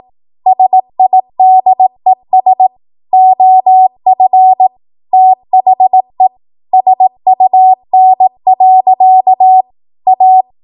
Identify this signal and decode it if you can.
Morse code, heard as a keyed tone, about 18 wpm, 750 Hz. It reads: SIDES OF THE SUN. A
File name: Morse / 18WPM